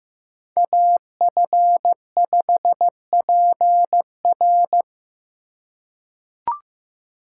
Morse code, heard as a keyed tone, about 16 wpm, 700 Hz. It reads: AF5PR
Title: USA Call Signs - 5 Characters - 4x Character Spacing - 15wpm